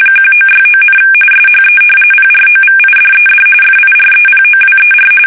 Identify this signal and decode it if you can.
RTTY, about 45.45 baud. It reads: 1969, SHE PROVIDED FIREPOWER
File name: fsk850 / battleshipnj